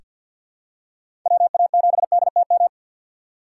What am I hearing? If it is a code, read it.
Morse code, heard as a keyed tone, about 47 wpm, 700 Hz. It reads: WA7DTM
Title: USA Call Signs - 4x Character Spacing - 50wpm